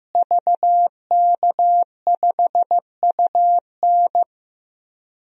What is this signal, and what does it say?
Morse code, heard as a keyed tone, about 15 wpm, 700 Hz. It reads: VK5UN
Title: DXCC-100 Call Signs - 5 Characters - Repeated 3x - 15wpm